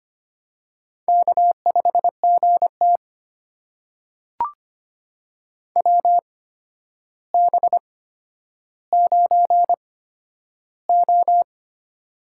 Morse code, heard as a keyed tone, about 25 wpm, spaced as for 11 wpm, 700 Hz. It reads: K5GT WB9O
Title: USA Call Signs - 4 Characters - 8x Character Spacing - 25wpm